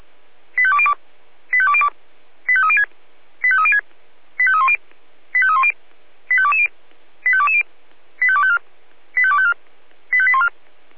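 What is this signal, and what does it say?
Five-tone selective call (ZVEI), heard as quick runs of five tones, tones of about 70 ms. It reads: 85202 85202 85306 85306 85310 85310 85220 85220 85254 85254 85814
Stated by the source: ZVEI five-tone selective call received on an SDR.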